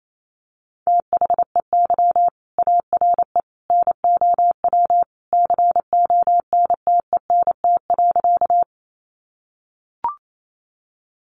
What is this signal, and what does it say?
Morse code, heard as a keyed tone, about 28 wpm, 700 Hz. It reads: THEY ARE NOW CONTENT.
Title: Sentences from Top 900 Words 28wpm